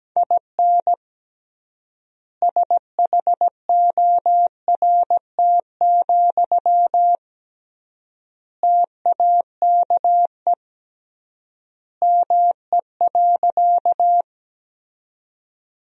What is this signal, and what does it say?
Morse code, heard as a keyed tone, about 17 wpm, 700 Hz. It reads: IN SHORT, TAKE ME.
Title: 3x Word-Spacing - Sentences from Top 400 Words - 17wpm